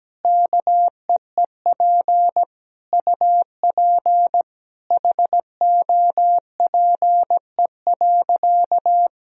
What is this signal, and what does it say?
Morse code, heard as a keyed tone, about 17 wpm, 700 Hz. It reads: KEEP UP HOPE.
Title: Sentences from Top 1000 Words - Review Part 1 - 17wpm